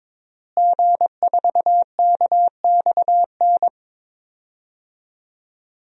Morse code, heard as a keyed tone, about 22 wpm, 700 Hz. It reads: G4KXN
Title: DXCC-100 Call Signs - 5 Characters - Repeated 2x - 22wpm